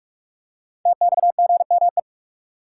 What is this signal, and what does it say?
Morse code, heard as a keyed tone, about 44 wpm, 700 Hz. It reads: TXGME
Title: Code Groups - 5 Letter:Numbers - 45wpm